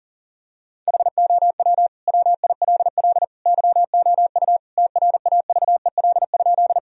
Morse code, heard as a keyed tone, about 37 wpm, 700 Hz. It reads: HOW WILL YOU TRAVEL?
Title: Sentences from Top 1000 Words - Review Part 2 - 40wpm